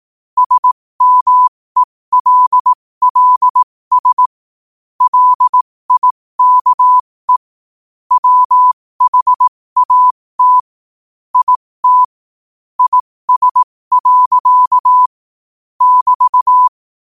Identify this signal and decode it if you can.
Morse code, heard as a keyed tone, about 18 wpm, spaced as for 15 wpm, 1000 Hz. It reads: SMELLS LIKE WHAT IT IS. <BT>
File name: qotdfiles / 15wpm